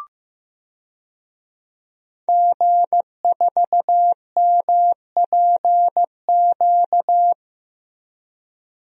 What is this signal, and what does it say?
Morse code, heard as a keyed tone, about 16 wpm, 700 Hz. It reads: G4MPQ
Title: DXCC-100 Call Signs - 5 Characters - Repeated 3x - 15wpm